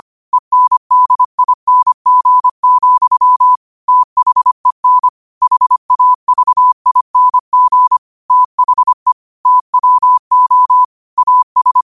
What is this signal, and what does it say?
Morse code, heard as a keyed tone, about 25 wpm, 1000 Hz. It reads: ENDING, THEN HAVING THE TWO AS
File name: qotdfiles / 25wpm